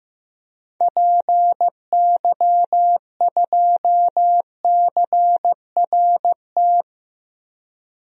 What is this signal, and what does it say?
Morse code, heard as a keyed tone, about 16 wpm, 700 Hz. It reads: PY2CRT
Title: DXCC-100 Call Signs - 6 Characters - 15wpm